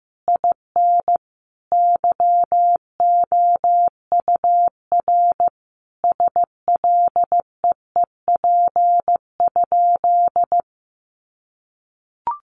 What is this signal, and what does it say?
Morse code, heard as a keyed tone, about 15 wpm, 700 Hz. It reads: IN YOUR SLEEP?
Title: Sentences from Top 500 Words 15wpm